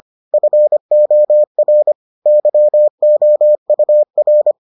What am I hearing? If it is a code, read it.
Morse code, heard as a keyed tone, about 25 wpm, 600 Hz. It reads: FOR YOUR